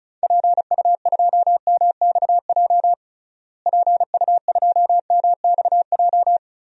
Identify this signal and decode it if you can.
Morse code, heard as a keyed tone, about 36 wpm, 700 Hz. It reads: PU2MXJ PU2MXJ
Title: DXCC-100 Call Signs - 6 Characters - Repeated 2x - 35wpm